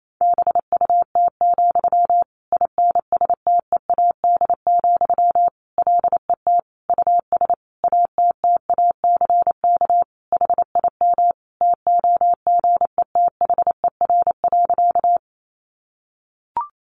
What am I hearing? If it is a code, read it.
Morse code, heard as a keyed tone, about 28 wpm, 700 Hz. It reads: BUT, INSTEAD, LET US ATTACK HIM TOGETHER.
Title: Sentences from Top 1500 Words 28wpm